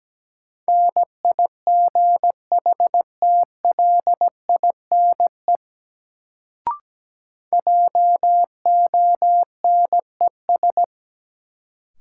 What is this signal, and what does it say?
Morse code, heard as a keyed tone, about 17 wpm, 700 Hz. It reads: NIGHTLINE JONES
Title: Sets of 5 Words - New Word Drill - 17wpm